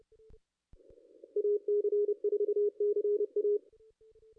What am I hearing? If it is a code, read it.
Morse code, heard as a keyed tone, about 31 wpm, 410 Hz. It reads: AC4CA
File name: contests / ss-cw-k5tr-op-n5rz -2200